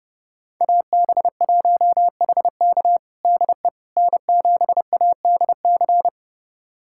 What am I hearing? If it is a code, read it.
Morse code, heard as a keyed tone, about 30 wpm, 700 Hz. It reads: AB1HK DE N7ADC
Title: USA Callsign DE Callsign - 5 Characters - 30wpm